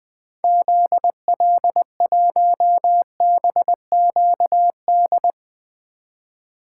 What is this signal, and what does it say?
Morse code, heard as a keyed tone, about 21 wpm, 700 Hz. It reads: ZL1BQD
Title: 2023 POTA Activator Call Signs - 20wpm